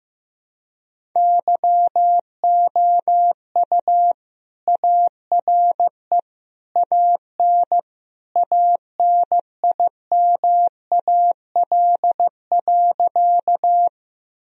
Morse code, heard as a keyed tone, about 15 wpm, 700 Hz. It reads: YOU ARE AN ANIMAL.
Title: Sentences from Top 1000 Words - Review Part 1 - 15wpm